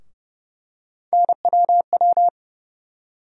Morse code, heard as a keyed tone, about 32 wpm, 700 Hz. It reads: NWW
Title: USA Call Sign Postfixes - 3 Letters 30wpm